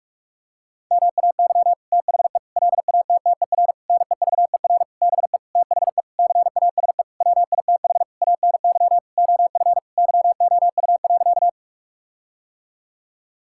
Morse code, heard as a keyed tone, about 46 wpm, 700 Hz. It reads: MAY THE LATTER NEVER BE THE CASE WITH ANY OF YOU.
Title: Sentences from Top 1600 Words 45wpm